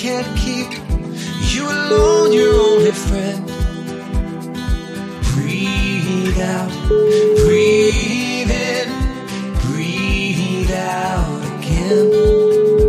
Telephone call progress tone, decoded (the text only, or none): ringback tone